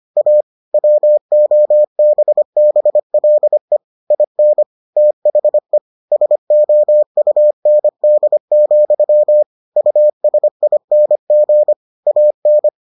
Morse code, heard as a keyed tone, about 25 wpm, 600 Hz. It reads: A WOBBLE IN THE SOUND, USING AN